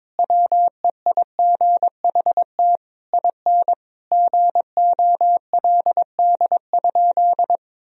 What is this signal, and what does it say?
Morse code, heard as a keyed tone, about 22 wpm, 700 Hz. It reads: WEIGHT IN GOLD?
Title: Sentences from Top 1000 Words - Review Part 3 - 22wpm